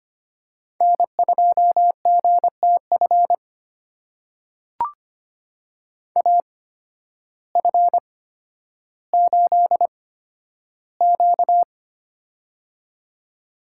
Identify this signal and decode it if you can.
Morse code, heard as a keyed tone, about 25 wpm, spaced as for 12 wpm, 700 Hz. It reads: N2GTF AF8Q
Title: USA Call Signs - 8x Character Spacing - 25wpm